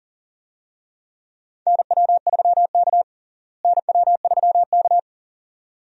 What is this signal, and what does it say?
Morse code, heard as a keyed tone, about 40 wpm, 700 Hz. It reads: NW3K NW3K
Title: USA Call Signs - 4 Characters - Repeated 2x - 40wpm